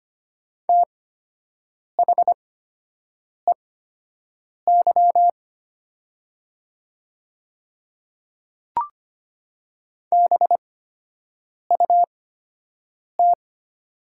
Morse code, heard as a keyed tone, about 25 wpm, spaced as for 7 wpm, 700 Hz. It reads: THEY BUT
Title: 8x Character Spacing - Top 100 Words - 25wpm